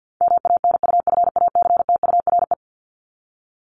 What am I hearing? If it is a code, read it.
Morse code, heard as a keyed tone, about 49 wpm, 700 Hz. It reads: MANUFACTURE
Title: ING Suffix - Top 100 Words with Base Words 50wpm